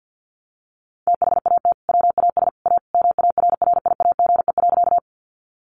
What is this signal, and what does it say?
Morse code, heard as a keyed tone, about 50 wpm, 700 Hz. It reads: THAT WAS A MARRIAGE.
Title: Sentences from Top 800 Words 50wpm